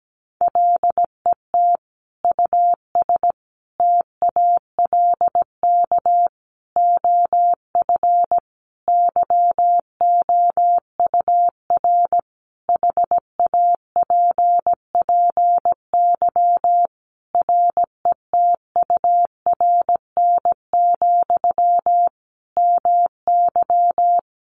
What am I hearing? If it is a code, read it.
Morse code, heard as a keyed tone, about 17 wpm, 700 Hz. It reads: LET US TALK OF YOUR HAPPY RETURN, MY
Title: Sentences from Top 1000 Words 17wpm